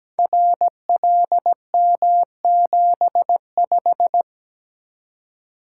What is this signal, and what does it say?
Morse code, heard as a keyed tone, about 17 wpm, 700 Hz. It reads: RLM75
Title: Code Groups - 5 Letter:Numbers - 17wpm